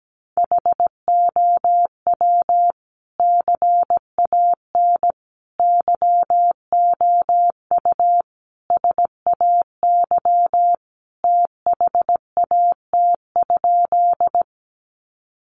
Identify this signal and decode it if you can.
Morse code, heard as a keyed tone, about 17 wpm, 700 Hz. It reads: HOW CAN YOU SAY THAT?
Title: Sentences from Top 200 Words 17wpm